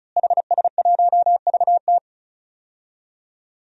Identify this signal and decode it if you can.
Morse code, heard as a keyed tone, about 35 wpm, 700 Hz. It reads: HS1VT
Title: DXCC-100 Call Signs - 5 Characters - Repeated 3x - 35wpm